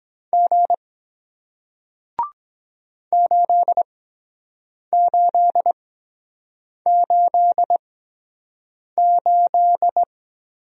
Morse code, heard as a keyed tone, about 23 wpm, spaced as for 7 wpm, 700 Hz. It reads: G 8888